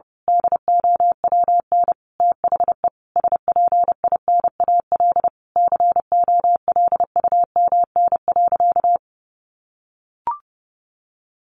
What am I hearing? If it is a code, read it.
Morse code, heard as a keyed tone, about 30 wpm, 700 Hz. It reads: DOWN THE SPINAL COLUMN.
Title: A Sound Of Thunder 30wpm